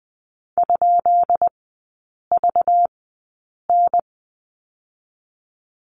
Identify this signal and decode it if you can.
Morse code, heard as a keyed tone, about 20 wpm, spaced as for 8 wpm, 690 Hz. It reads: ?VN
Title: 212 Lesson 34e - Number 8 - Three Characters 20wpm